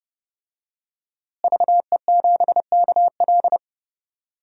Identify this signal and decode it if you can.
Morse code, heard as a keyed tone, about 30 wpm, 700 Hz. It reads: VE7KL
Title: DXCC-100 Call Signs - 5 Characters - Repeated 3x - 30wpm